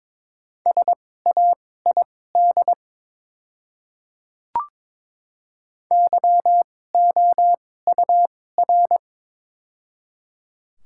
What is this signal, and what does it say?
Morse code, heard as a keyed tone, about 22 wpm, spaced as for 16 wpm, 700 Hz. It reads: SAID YOUR
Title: 2x Character Spacing - Top 100 Words - 22wpm